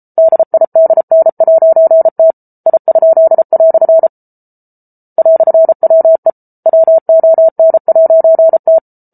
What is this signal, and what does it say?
Morse code, heard as a keyed tone, about 34 wpm, 650 Hz. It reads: DIDN'T I?" "WE WON'T